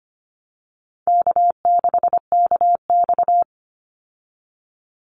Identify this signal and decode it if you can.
Morse code, heard as a keyed tone, about 25 wpm, 700 Hz. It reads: K6KX
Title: Easy Call Signs Repeated 3x - 25wpm